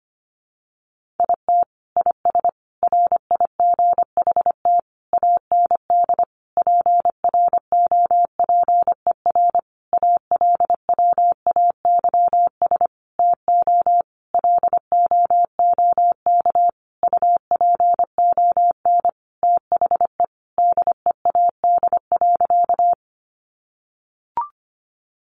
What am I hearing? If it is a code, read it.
Morse code, heard as a keyed tone, about 25 wpm, 700 Hz. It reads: IT IS RIGHT AND PROPER ALWAYS TO LOOK UPON THE DEAD.
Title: Sentences from Top 1500 Words 25wpm